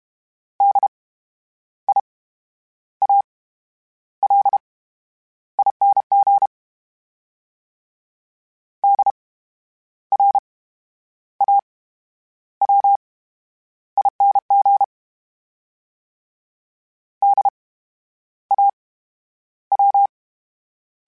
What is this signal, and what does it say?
Morse code, heard as a keyed tone, about 32 wpm, 800 Hz. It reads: D I A L I N G D R A W I N G D A W
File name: Special INGs / Special INGs-mp3s / VerySlowToFaster Ing @30